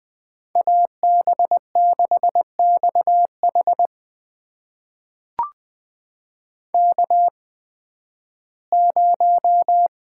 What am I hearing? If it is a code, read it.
Morse code, heard as a keyed tone, about 20 wpm, spaced as for 13 wpm, 700 Hz. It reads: AB6XH K0
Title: USA Call Signs - 8x Character Spacing - 20wpm